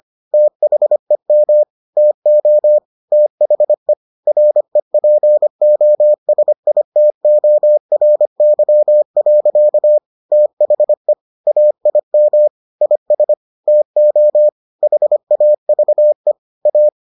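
Morse code, heard as a keyed tone, about 25 wpm, 600 Hz. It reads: THEM TO THE REPOSITORY. THE AIM IS TO HAVE A
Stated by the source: Morse edition of the Foundations of Amateur Radio episodes.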